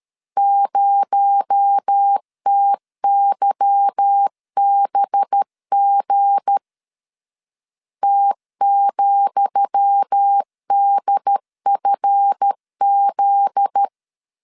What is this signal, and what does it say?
Morse code, heard as a keyed tone, about 13 wpm, 800 Hz. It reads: 0TYBG T,DFZ
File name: LEZIONI / 10 SIGNS / 10D COMA 12wpm 2544